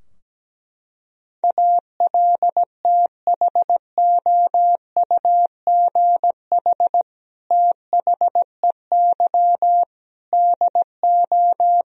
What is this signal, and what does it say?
Morse code, heard as a keyed tone, about 17 wpm, 700 Hz. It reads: ALTHOUGH THEY DO
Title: Sentences from Top 1000 Words - Review Part 3 - 17wpm